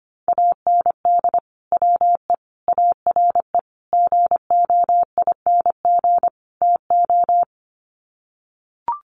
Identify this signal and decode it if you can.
Morse code, heard as a keyed tone, about 25 wpm, 700 Hz. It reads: AND WE ARE GOING TO
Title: Sets of 5 Words Encore2 25wpm